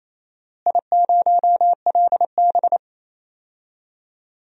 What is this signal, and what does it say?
Morse code, heard as a keyed tone, about 27 wpm, 700 Hz. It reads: I0LB
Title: DXCC-100 Call Signs - 4 Characters - Repeated 3x - 28wpm